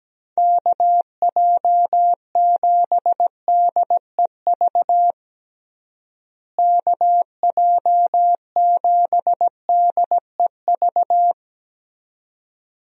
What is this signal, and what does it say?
Morse code, heard as a keyed tone, about 17 wpm, 700 Hz. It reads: KJ7DEV KJ7DEV
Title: USA Call Signs - 6 Characters - Repeated 2x - 17wpm